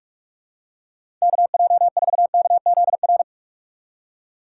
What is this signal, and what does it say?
Morse code, heard as a keyed tone, about 46 wpm, 700 Hz. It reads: KJ4KZR
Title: USA Call Signs - 6 Characters - 6x Character Spacing - 45wpm